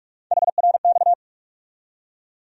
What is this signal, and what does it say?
Morse code, heard as a keyed tone, about 47 wpm, 700 Hz. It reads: HRX
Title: USA Call Sign Postfixes - 3 Letters 45wpm